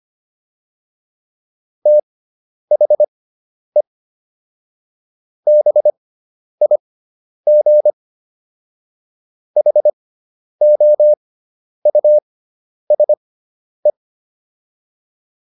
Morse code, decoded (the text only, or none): THE BIG HOUSE